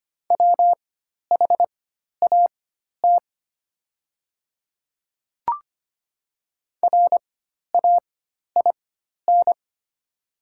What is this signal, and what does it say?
Morse code, heard as a keyed tone, about 25 wpm, spaced as for 12 wpm, 700 Hz. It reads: WHAT RAIN
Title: 4x Character Spacing - Top 300 Words - 25wpm